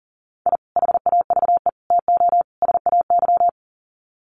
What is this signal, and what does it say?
Morse code, heard as a keyed tone, about 40 wpm, 700 Hz. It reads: I HAVE TO SAY
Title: Sets of 4 Words 40wpm